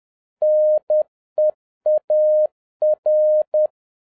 Morse code, heard as a keyed tone, about 10 wpm, 610 Hz. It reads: NEAR
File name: 2011-07-mp3 / n31 THE 3.11 2 10wpm